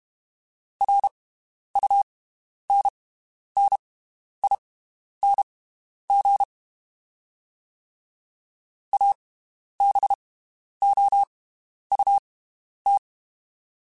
Morse code, heard as a keyed tone, about 32 wpm, spaced as for 11 wpm, 800 Hz. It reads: RUNNING ABOUT
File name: (spaced) / QSOs Spaced-mp3s / WD0FBT 1 @30 sp